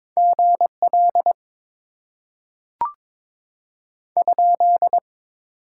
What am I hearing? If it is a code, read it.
Morse code, heard as a keyed tone, about 21 wpm, 700 Hz. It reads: GL ?
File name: Top 90 QSO Elements - 6x Character Spacing - 22wpm